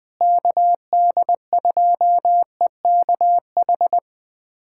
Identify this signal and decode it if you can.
Morse code, heard as a keyed tone, about 20 wpm, 700 Hz. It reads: KD2EKH
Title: USA Callsign DE Callsign 20wpm